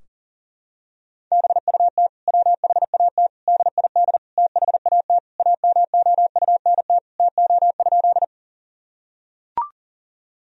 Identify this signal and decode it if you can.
Morse code, heard as a keyed tone, about 40 wpm, 700 Hz. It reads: BUT WHAT DID THAT AMOUNT TO?
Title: Sentences from Top 1000 Words - Review Part 4 - 40wpm